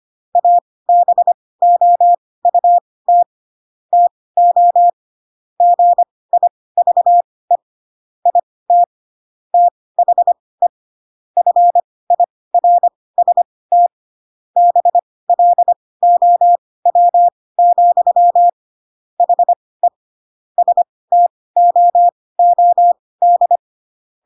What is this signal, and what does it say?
Morse code, heard as a keyed tone, about 25 wpm, spaced as for 18 wpm, 700 Hz. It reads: ABOUT TO GIVE IT THE FIRST BLOW, HE STOOD
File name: Pinocchio / Pinocchio ch1 12 25 ik1hge